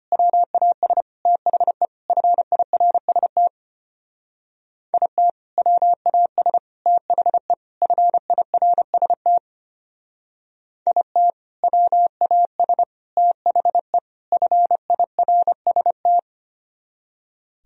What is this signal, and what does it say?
Morse code, heard as a keyed tone, about 33 wpm, 700 Hz. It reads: WAS THE FIRST IT WAS THE FIRST IT WAS THE FIRST